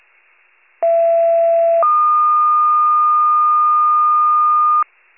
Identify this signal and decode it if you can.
Two-tone page, heard as a one-second tone, then a three-second tone, about 670 then 1220 Hz. It is heard from 0.8 to 4.8 s.